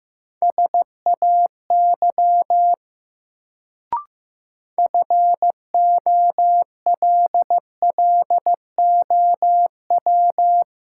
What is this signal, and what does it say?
Morse code, heard as a keyed tone, about 15 wpm, 700 Hz. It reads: SAY FOLLOW
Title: Sentences from Top 200 Words - New Word Drill - 15wpm